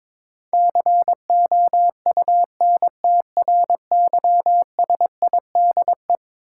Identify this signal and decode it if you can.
Morse code, heard as a keyed tone, about 22 wpm, 700 Hz. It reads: COUNTRYSIDE
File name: practice / 22wpm / Eleven Letter Words - 22wpm